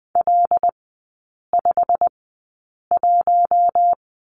Morse code, heard as a keyed tone, about 20 wpm, spaced as for 8 wpm, 700 Hz. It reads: L51
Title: 087 Lesson 15e - Number 5 - Three Characters 20wpm